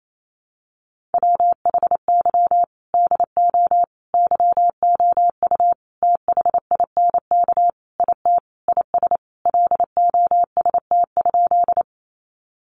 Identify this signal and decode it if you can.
Morse code, heard as a keyed tone, about 28 wpm, 700 Hz. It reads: WHY DO YOU THINK IT IS LOST?
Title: Sentences from Top 1400 Words 28wpm